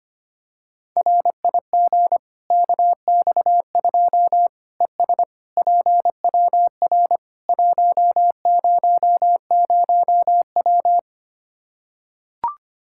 Morse code, heard as a keyed tone, about 25 wpm, 700 Hz. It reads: RIG KX2 ES PWR 100W
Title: K1USN SST QSO Practice 25wpm